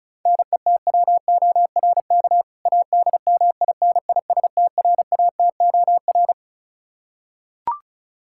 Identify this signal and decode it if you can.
Morse code, heard as a keyed tone, about 34 wpm, 700 Hz. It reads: NETWORK ADMINISTRATOR
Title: K1USN SST QSO Practice 35wpm